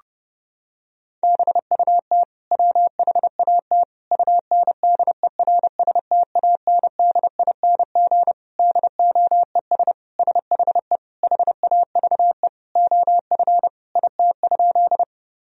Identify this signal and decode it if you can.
Morse code, heard as a keyed tone, about 31 wpm, 700 Hz. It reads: BUT WHAT UNDERSTANDING DOES SHE HAVE OF IT?
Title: Sentences from Top 1000 Words - Review Part 3 - 30wpm